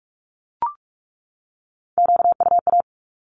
Morse code, heard as a keyed tone, about 44 wpm, 700 Hz. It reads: QUA
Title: Q-Codes 45wpm